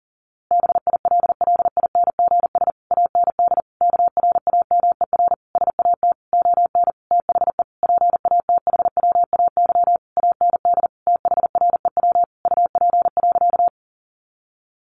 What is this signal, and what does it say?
Morse code, heard as a keyed tone, about 40 wpm, 700 Hz. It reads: BILLINGS AND KRAMER SAT ON THE PATHWAY AND THREW UP.
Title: A Sound Of Thunder 40wpm